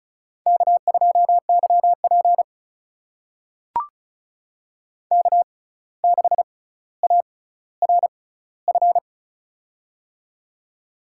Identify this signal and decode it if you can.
Morse code, heard as a keyed tone, about 36 wpm, spaced as for 17 wpm, 700 Hz. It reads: K2YP K6ARF
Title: USA Call Signs - 6x Character Spacing - 35wpm